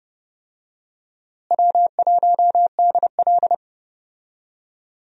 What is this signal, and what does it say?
Morse code, heard as a keyed tone, about 30 wpm, 700 Hz. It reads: W1DL
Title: USA Call Signs - 4 Characters - Repeated 2x - 30wpm